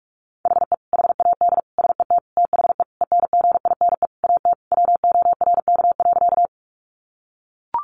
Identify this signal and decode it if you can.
Morse code, heard as a keyed tone, about 44 wpm, 700 Hz. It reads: HE HAD SET THE ENGINE AT WORK.
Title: Sentences from Top 1500 Words 45wpm